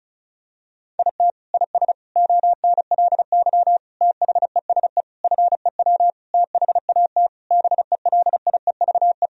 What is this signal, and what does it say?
Morse code, heard as a keyed tone, about 36 wpm, 700 Hz. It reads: IT IS ONLY THESE FEW THAT BELIEVE
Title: Sentences from Top 1000 Words - Review Part 1 - 35wpm